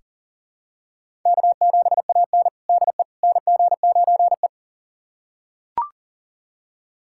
Morse code, heard as a keyed tone, about 40 wpm, 700 Hz. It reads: K7AN DE NG9E
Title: USA Callsign DE Callsign - 4 Characters - 40wpm